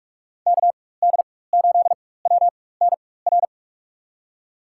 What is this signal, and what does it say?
Morse code, heard as a keyed tone, about 44 wpm, spaced as for 21 wpm, 700 Hz. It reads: KD8WNR
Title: USA Call Signs - 6 Characters - 4x Character Spacing - 45wpm